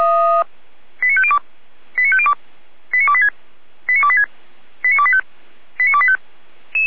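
Five-tone selective call (ZVEI), heard as quick runs of five tones, tones of about 70 ms. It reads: 89502 89502 89286 89286 89285 89285
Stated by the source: ZVEI five-tone selective call received by SDR, from the feuerwehr-orlamuende.org (Orlamünde fire brigade) audio server.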